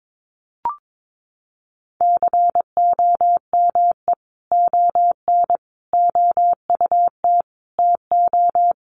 Morse code, heard as a keyed tone, about 22 wpm, 700 Hz. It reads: COME ON OUT TO
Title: Sentences from Top 1500 Words 22wpm